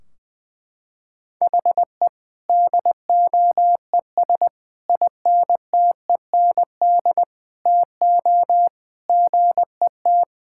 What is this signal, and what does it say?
Morse code, heard as a keyed tone, about 20 wpm, 700 Hz. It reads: HE DOES INTEND TO GET
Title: Sentences from Top 1600 Words 20wpm